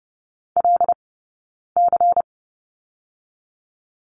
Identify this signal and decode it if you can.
Morse code, heard as a keyed tone, about 30 wpm, 700 Hz. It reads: L C
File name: Lesson 17d - Letter u - Two Characters 30wpm